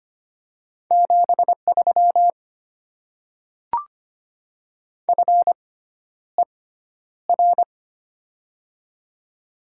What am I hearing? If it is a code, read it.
Morse code, heard as a keyed tone, about 25 wpm, spaced as for 11 wpm, 690 Hz. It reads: 73 FER
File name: Top 20 QSO Elements - 6x Character Spacing - 25wpm